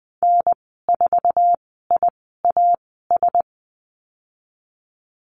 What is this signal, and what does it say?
Morse code, heard as a keyed tone, about 20 wpm, spaced as for 15 wpm, 700 Hz. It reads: N4IAS
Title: 055 Lesson 10g - Letter r - Call Signs 20wpm